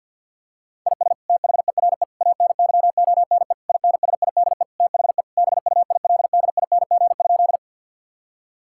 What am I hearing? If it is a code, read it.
Morse code, heard as a keyed tone, about 48 wpm, 700 Hz. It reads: IS THERE ANYONE INSIDE THE BUILDING?
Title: Sentences from Top 1000 Words - Review Part 6 - 50wpm